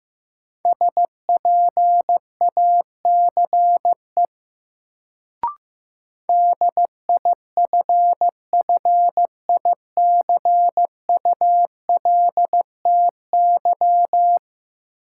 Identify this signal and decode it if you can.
Morse code, heard as a keyed tone, about 15 wpm, 700 Hz. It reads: SPACE DIFFICULTY
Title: Sentences from Top 1500 Words - New Word Drill Alt - 15wpm